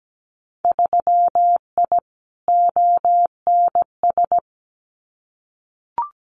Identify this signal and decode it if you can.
Morse code, heard as a keyed tone, about 17 wpm, 700 Hz. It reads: 3I ONS
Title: Winter Field Day Exchanges Encore 17wpm